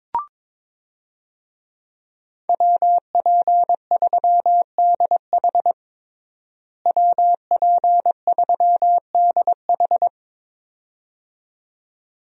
Morse code, heard as a keyed tone, about 22 wpm, 700 Hz. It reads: WP3DH WP3DH
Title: USA Call Signs - 5 Characters - Repeated 2x - 22wpm